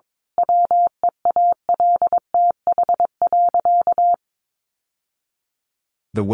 Morse code, heard as a keyed tone, about 22 wpm, 700 Hz. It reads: WEALTH.
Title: Sentences from Top 1200 Words 22wpm